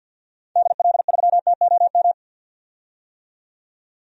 Morse code, heard as a keyed tone, about 52 wpm, 700 Hz. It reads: DL3TOM
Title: DXCC-100 Call Signs - 6 Characters - 50wpm